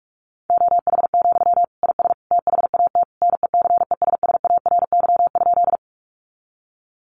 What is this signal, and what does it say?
Morse code, heard as a keyed tone, about 44 wpm, 700 Hz. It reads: OH, IS THAT NECESSARY?
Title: Sentences from Top 800 Words 45wpm